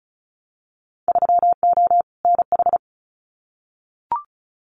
Morse code, heard as a keyed tone, about 36 wpm, 700 Hz. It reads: 3O NH
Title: Winter Field Day Exchanges Encore 35wpm